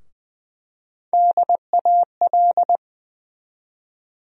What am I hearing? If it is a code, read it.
Morse code, heard as a keyed tone, about 20 wpm, 700 Hz. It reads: DAL
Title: USA Call Sign Postfixes - 3 Letters 20wpm